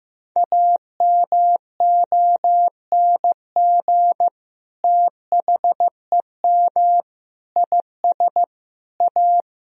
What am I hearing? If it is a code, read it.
Morse code, heard as a keyed tone, about 15 wpm, 700 Hz. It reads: AMONG THEM IS A
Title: Sentences from Top 1000 Words - Review Part 3 - 15wpm